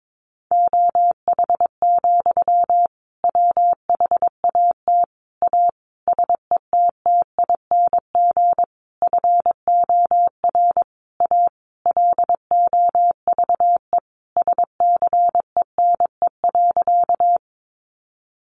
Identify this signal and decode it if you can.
Morse code, heard as a keyed tone, about 22 wpm, 700 Hz. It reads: OH, WHAT A SETTING FOR A LOVE SCENE.
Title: Sentences from Top 1300 Words 22wpm